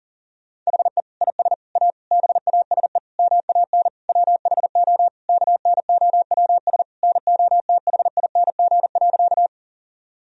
Morse code, heard as a keyed tone, about 40 wpm, 700 Hz. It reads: HE IS A BASE MAN WHO KNOWS NOTHING.
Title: Sentences from Top 1000 Words - Review Part 1 - 40wpm